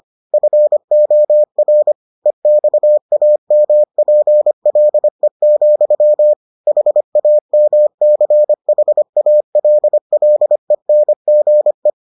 Morse code, heard as a keyed tone, about 25 wpm, 600 Hz. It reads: FOR EXAMPLE, HAMCHALLENGE